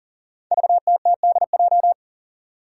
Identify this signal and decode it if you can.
Morse code, heard as a keyed tone, about 43 wpm, 700 Hz. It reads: VTTDJ
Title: Five Challenging Letter-Number Combinations 40wpm